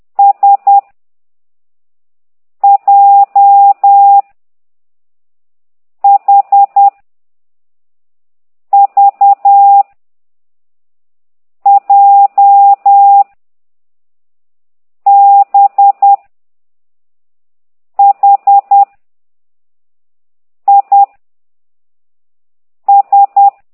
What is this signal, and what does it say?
Morse code, heard as a keyed tone, about 10 wpm, 800 Hz. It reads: S J H V J B H I S